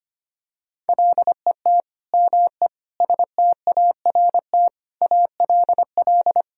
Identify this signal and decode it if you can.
Morse code, heard as a keyed tone, about 25 wpm, 700 Hz. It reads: LET ME START ALL
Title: Sentences from Top 1000 Words - Review Part 1 - 25wpm